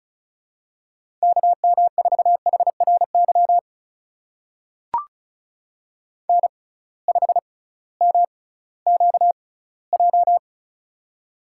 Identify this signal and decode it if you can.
Morse code, heard as a keyed tone, about 36 wpm, spaced as for 19 wpm, 700 Hz. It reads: KM4HRY N5MQJ
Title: USA Call Signs - 6x Character Spacing - 35wpm